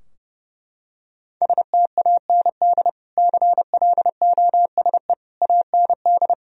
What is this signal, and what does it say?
Morse code, heard as a keyed tone, about 30 wpm, 700 Hz. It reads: STAND CLOSE AND
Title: Sentences from Top 1000 Words - Review Part 2 - 30wpm